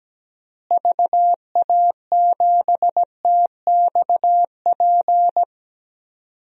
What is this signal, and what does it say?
Morse code, heard as a keyed tone, about 17 wpm, 700 Hz. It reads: VA7TXP
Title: DXCC-100 Call Signs - 6 Characters - Repeated 2x - 17wpm